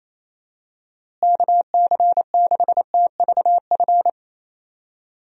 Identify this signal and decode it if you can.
Morse code, heard as a keyed tone, about 28 wpm, 700 Hz. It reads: KC6TVF
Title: USA Call Signs - 6 Characters - Repeated 2x - 28wpm